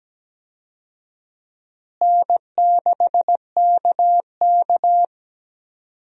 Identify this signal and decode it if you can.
Morse code, heard as a keyed tone, about 17 wpm, 700 Hz. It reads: N6KK
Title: USA Call Signs - 4 Characters - Repeated 2x - 17wpm